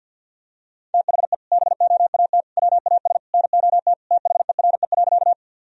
Morse code, heard as a keyed tone, about 49 wpm, 700 Hz. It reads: THE BOAT WAS NOT THERE.
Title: Sentences from Top 1000 Words - Review Part 3 - 50wpm